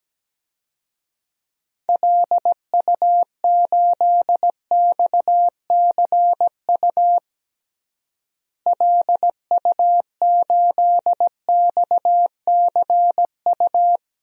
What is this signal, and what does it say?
Morse code, heard as a keyed tone, about 17 wpm, 700 Hz. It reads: LU8XCU LU8XCU
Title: DXCC-100 Call Signs - 6 Characters - Repeated 2x - 17wpm